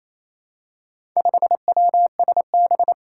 Morse code, heard as a keyed tone, about 28 wpm, 700 Hz. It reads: 5WSB
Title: Four Challenging Letter-Number Combinations 28wpm